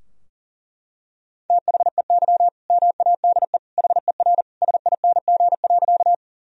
Morse code, heard as a keyed tone, about 40 wpm, 700 Hz. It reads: THEY MADE HER SING.
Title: Sentences from Top 1000 Words - Review Part 2 - 40wpm